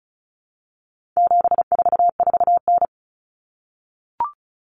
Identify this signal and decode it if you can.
Morse code, heard as a keyed tone, about 34 wpm, 700 Hz. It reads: 744N
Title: Common Cut Numbers - 4-Digit 35wpm